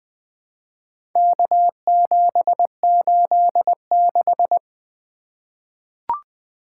Morse code, heard as a keyed tone, about 20 wpm, 700 Hz. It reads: K786
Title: Four Letter-Number Combinations - 20wpm